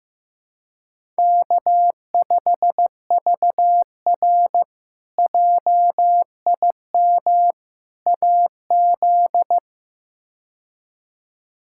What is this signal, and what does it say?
Morse code, heard as a keyed tone, about 15 wpm, 700 Hz. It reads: K5VR JIM AZ
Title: K1USN SST QSO Practice - 15wpm